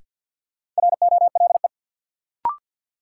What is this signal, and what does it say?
Morse code, heard as a keyed tone, about 50 wpm, 700 Hz. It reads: ROLE
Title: Sentences from Top 1000 Words - Review Part 6 - New Word Drill - 50wpm